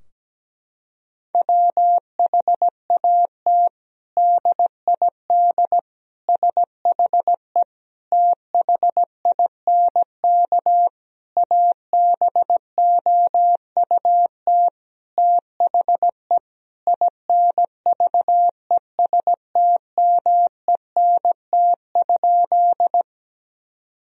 Morse code, heard as a keyed tone, about 17 wpm, 700 Hz. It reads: WHAT DID SHE THINK ABOUT THE INVESTMENT?
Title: Sentences from Top 1000 Words - Review Part 4 - 17wpm